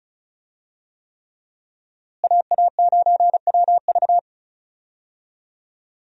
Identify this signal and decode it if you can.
Morse code, heard as a keyed tone, about 36 wpm, 700 Hz. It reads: AA9WV
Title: USA Call Signs - 5 Characters - Repeated 3x - 35wpm